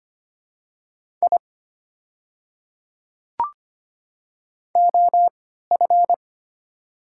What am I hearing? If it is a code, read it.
Morse code, heard as a keyed tone, about 25 wpm, spaced as for 14 wpm, 700 Hz. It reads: I OF
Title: 3x Character Spacing - Top 200 Words - 25wpm